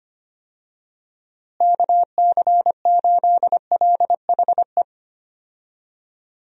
Morse code, heard as a keyed tone, about 25 wpm, 700 Hz. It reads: KC8LHE
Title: USA Call Signs - 6 Characters - Repeated 2x - 25wpm